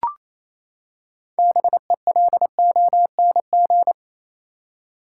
Morse code, heard as a keyed tone, about 28 wpm, 700 Hz. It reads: BELONG
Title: Six Letter Words - 28wpm